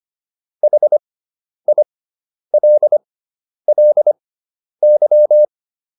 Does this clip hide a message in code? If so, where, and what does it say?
Yes: Morse code from 0.6 to 5.5 s, heard as a keyed tone, about 25 wpm, spaced as for 10 wpm, 600 Hz. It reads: HILLY